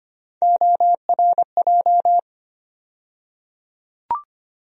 Morse code, heard as a keyed tone, about 25 wpm, 700 Hz. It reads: ORJ
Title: USA Call Sign Postfixes - 3 Letters 25wpm